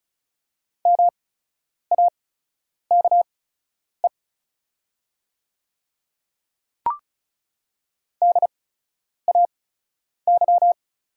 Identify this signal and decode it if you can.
Morse code, heard as a keyed tone, about 36 wpm, spaced as for 10 wpm, 700 Hz. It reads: MAKE DAY
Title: 8x Character Spacing - Top 300 Words - 35wpm